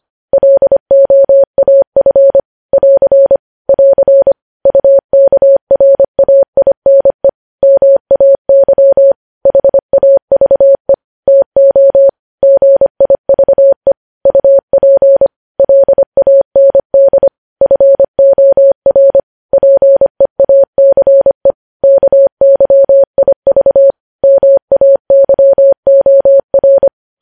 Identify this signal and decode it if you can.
Morse code, heard as a keyed tone, about 25 wpm, 550 Hz. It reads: LOAF <AR> <AR> UKRAINE MAY HAVE TO GIVE UP LAND FOR PEACE KYIV MAYOR
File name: podcast / News headlines-1x-25x25